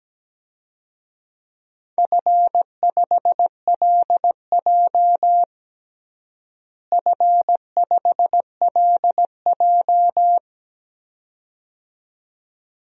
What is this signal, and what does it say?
Morse code, heard as a keyed tone, about 17 wpm, 700 Hz. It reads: F5LJ F5LJ
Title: DXCC-100 Call Signs - 4 Characters - Repeated 2x - 17wpm